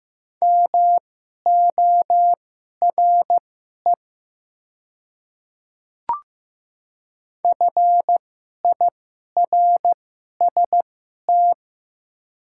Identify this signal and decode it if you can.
Morse code, heard as a keyed tone, about 15 wpm, spaced as for 11 wpm, 700 Hz. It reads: MORE FIRST
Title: 2x Character Spacing - Top 200 Words - 15wpm